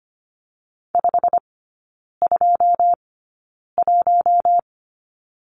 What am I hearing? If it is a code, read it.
Morse code, heard as a keyed tone, about 25 wpm, spaced as for 9 wpm, 690 Hz. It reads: 521
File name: Lesson 15e - Number 5 - Three Characters 25wpm